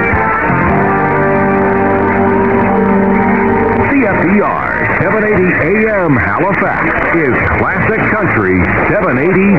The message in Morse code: none

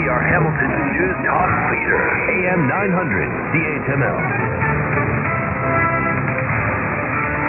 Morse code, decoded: none

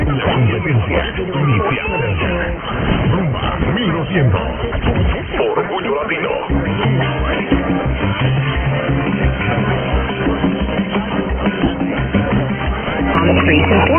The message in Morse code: none

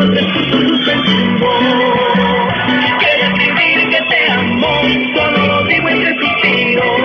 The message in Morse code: none

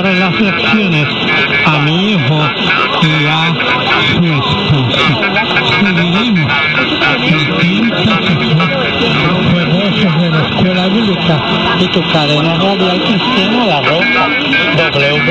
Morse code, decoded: none

